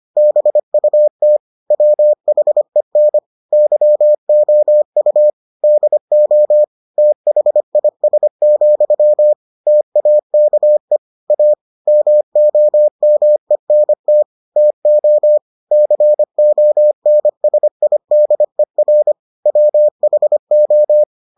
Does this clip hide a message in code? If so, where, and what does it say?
Yes: Morse code from 0.2 to 21.0 s, heard as a keyed tone, about 25 wpm, 600 Hz. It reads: BUT WHEN YOU DO THIS, TAKE A MOMENT TO CONSIDER WHO